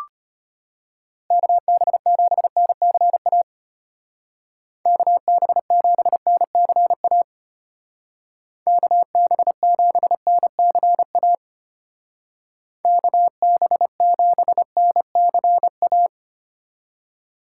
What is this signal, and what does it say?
Morse code, heard as a keyed tone, about 31 wpm, 700 Hz. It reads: KB7NCA KB7NCA KB7NCA KB7NCA